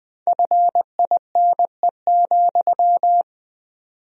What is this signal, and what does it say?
Morse code, heard as a keyed tone, about 20 wpm, 700 Hz. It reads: FINE,
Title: 4x Word-Spacing - Sentences from Top 500 Words - 20wpm